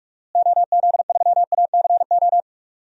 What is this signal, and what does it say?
Morse code, heard as a keyed tone, about 46 wpm, 700 Hz. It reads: OZ3ACO
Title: DXCC-100 Call Signs - 6 Characters - Repeated 3x - 45wpm